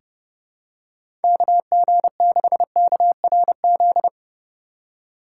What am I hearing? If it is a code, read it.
Morse code, heard as a keyed tone, about 30 wpm, 700 Hz. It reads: KG6KRZ
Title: USA Call Signs - 6 Characters - 8x Character Spacing - 30wpm